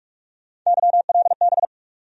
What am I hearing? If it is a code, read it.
Morse code, heard as a keyed tone, about 46 wpm, 700 Hz. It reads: YLB
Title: USA Call Sign Postfixes - 3 Letters 45wpm